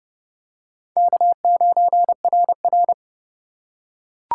Morse code, heard as a keyed tone, about 28 wpm, 700 Hz. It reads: K9RR